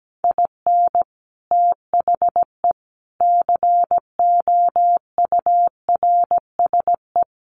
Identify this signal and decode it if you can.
Morse code, heard as a keyed tone, about 17 wpm, 700 Hz. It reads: IN THE COURSE
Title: Sets of 4 Words 17wpm